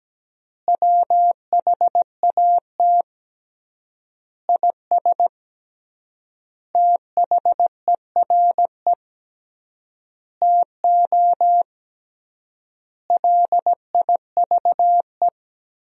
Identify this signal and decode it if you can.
Morse code, heard as a keyed tone, about 17 wpm, 700 Hz. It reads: WHAT IS THERE TO LIVE
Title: 3x Word-Spacing - Sentences from Top 200 Words - 17wpm